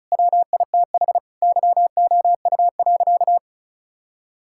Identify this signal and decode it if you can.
Morse code, heard as a keyed tone, about 34 wpm, 690 Hz. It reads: WITH YOU.
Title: Sentences from Top 1000 Words - Review Part 5 - 35wpm